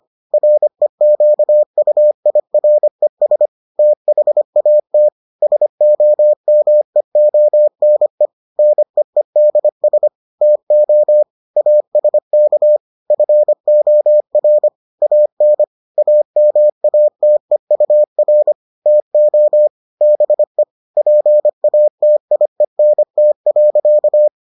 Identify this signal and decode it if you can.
Morse code, heard as a keyed tone, about 25 wpm, 600 Hz. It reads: REQUIRES THAT SOMEONE NEEDS TO ASK FOR AN AMATEUR TO BE PATIENT.